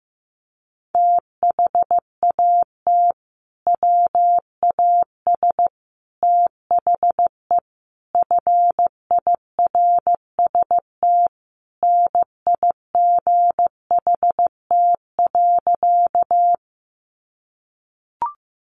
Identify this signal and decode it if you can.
Morse code, heard as a keyed tone, about 15 wpm, 700 Hz. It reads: THAT WAS THE FIRST NIGHT.
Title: Sentences from Top 300 Words 15wpm